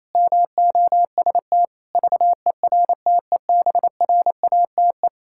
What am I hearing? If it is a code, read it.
Morse code, heard as a keyed tone, about 28 wpm, 700 Hz. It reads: MOST VERTEBRATE
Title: Pangram Challenge v2 - 28wpm